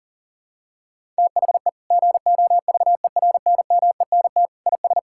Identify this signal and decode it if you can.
Morse code, heard as a keyed tone, about 40 wpm, 700 Hz. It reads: THE GOVERNMENT IS
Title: Sentences from Top 1000 Words - Review Part 5 - 40wpm